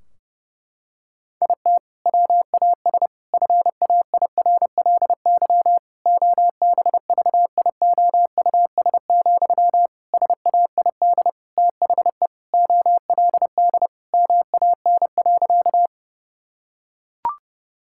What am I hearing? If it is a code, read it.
Morse code, heard as a keyed tone, about 32 wpm, 700 Hz. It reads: IT WAS FAIRLY OBVIOUS, SAID THE OLD MAN.
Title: Sentences from Top 1600 Words 30wpm